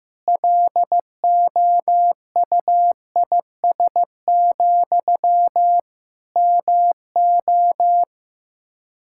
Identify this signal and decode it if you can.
Morse code, heard as a keyed tone, about 15 wpm, 700 Hz. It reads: LOUIS, MO
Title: Practice QSO - Exchange 1 - Beginning - 15wpm